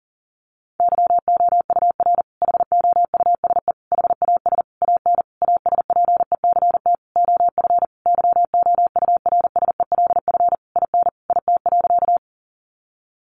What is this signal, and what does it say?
Morse code, heard as a keyed tone, about 40 wpm, 700 Hz. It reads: YOUR HOUSE HAS AN ASPECT OF YOURSELF IN IT.
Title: Sentences from Top 1300 Words 40wpm